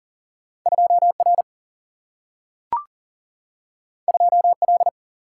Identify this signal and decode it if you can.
Morse code, heard as a keyed tone, about 41 wpm, 700 Hz. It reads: 2R 2L
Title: Single-Letter Single-Number 40wpm